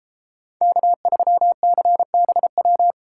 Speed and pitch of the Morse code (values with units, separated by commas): 31 wpm, 700 Hz